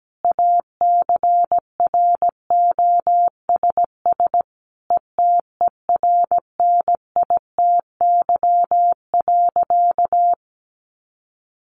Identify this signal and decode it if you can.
Morse code, heard as a keyed tone, about 17 wpm, 700 Hz. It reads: ACROSS ETERNITY.
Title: A Sound Of Thunder 17wpm